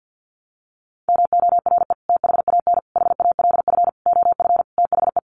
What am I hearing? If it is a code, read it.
Morse code, heard as a keyed tone, about 50 wpm, 700 Hz. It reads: MORE THAN HALF OF THE
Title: Sets of 5 Words 50wpm